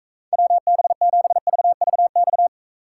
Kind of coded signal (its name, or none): Morse code